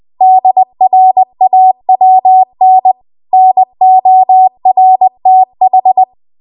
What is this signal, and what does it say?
Morse code, heard as a keyed tone, about 20 wpm, 750 Hz. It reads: DRAWN NORTH
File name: Archive / 20 WPM / 20WPM